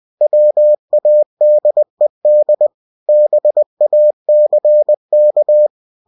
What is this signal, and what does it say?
Morse code, heard as a keyed tone, about 20 wpm, 600 Hz. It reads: WADED BACK
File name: cw-books / winnie-the-pooh / winnie-20wpm-page-0100